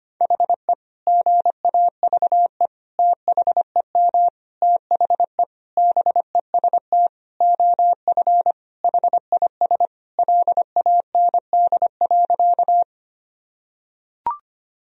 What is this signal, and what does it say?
Morse code, heard as a keyed tone, about 25 wpm, 700 Hz. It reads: HE GAVE THEM THE BEST OF HIS LAND.
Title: Sentences from Top 1000 Words - Review Part 1 - 25wpm